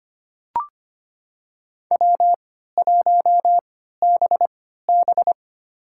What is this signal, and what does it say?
Morse code, heard as a keyed tone, about 25 wpm, spaced as for 14 wpm, 700 Hz. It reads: W1BB
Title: USA Call Signs - 4 Characters - 3x Character Spacing - 25wpm